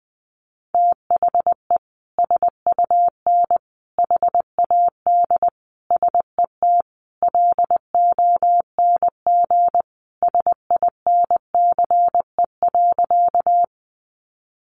Morse code, heard as a keyed tone, about 20 wpm, 700 Hz. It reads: THE SUN HAD SET LONG SINCE.
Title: Sentences from Top 300 Words 20wpm